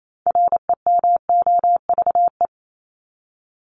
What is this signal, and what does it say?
Morse code, heard as a keyed tone, about 28 wpm, 700 Hz. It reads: REMOVE
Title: ED Suffix - Top 100 Words with Base Words 28wpm